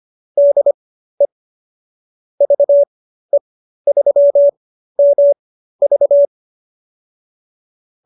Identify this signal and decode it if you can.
Morse code, heard as a keyed tone, about 25 wpm, spaced as for 13 wpm, 570 Hz. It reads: DE VE3MV